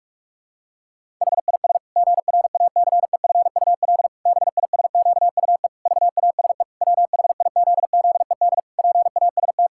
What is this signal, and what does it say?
Morse code, heard as a keyed tone, about 45 wpm, 700 Hz. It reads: HIS GRACEFUL BISQUE VASE WHIZZED PAST
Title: Pangram Challenge v2 - 45wpm